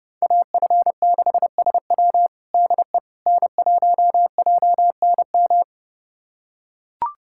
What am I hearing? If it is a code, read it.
Morse code, heard as a keyed tone, about 30 wpm, 700 Hz. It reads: AF6SW DE N1JNM
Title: USA Callsign DE Callsign - 5 Characters - 30wpm